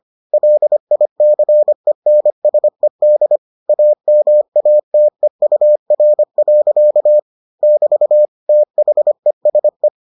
Morse code, heard as a keyed tone, about 25 wpm, 600 Hz. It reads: LICENSED AMATEUR. <BT> THESE